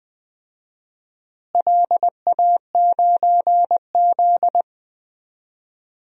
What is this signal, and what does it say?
Morse code, heard as a keyed tone, about 21 wpm, 700 Hz. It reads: LA9Z
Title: DXCC-100 Call Signs - 4 Characters - Repeated 2x - 20wpm